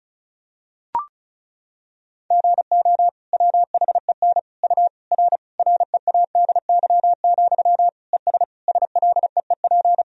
Morse code, heard as a keyed tone, about 36 wpm, 700 Hz. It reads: GO WHEN U R READY, ES SLEEP
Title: Sentences from Top 500 Words and Abbreviations 35wpm